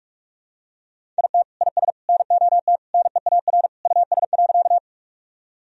Morse code, heard as a keyed tone, about 46 wpm, 700 Hz. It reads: IT IS NOT NEAR US.
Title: Sentences from Top 1000 Words - Review Part 1 - 45wpm